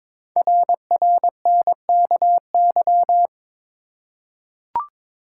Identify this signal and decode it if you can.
Morse code, heard as a keyed tone, about 23 wpm, 700 Hz. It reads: RRNKY
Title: Code Groups - 5 Letters - 22wpm